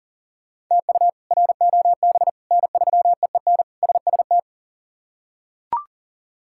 Morse code, heard as a keyed tone, about 40 wpm, 700 Hz. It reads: TU ROB N3EEN SST
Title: K1USN SST QSO Practice - 40wpm